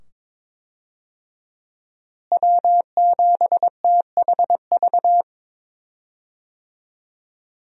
Morse code, heard as a keyed tone, about 22 wpm, 700 Hz. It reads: W7THV
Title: USA Call Signs - 5 Characters - Repeated 2x - 22wpm